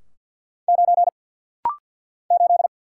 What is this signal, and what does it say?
Morse code, heard as a keyed tone, about 48 wpm, 700 Hz. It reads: 9 8
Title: Single Letter-Number - Rapid-Fire 50wpm